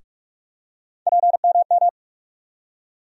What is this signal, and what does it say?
Morse code, heard as a keyed tone, about 46 wpm, 700 Hz. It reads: PMM
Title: USA Call Sign Postfixes - 3 Letters 45wpm